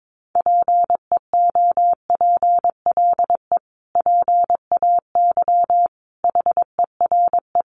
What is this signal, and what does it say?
Morse code, heard as a keyed tone, about 22 wpm, 700 Hz. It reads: PEOPLE PAY HERE
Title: Sentences from Top 700 Words 22wpm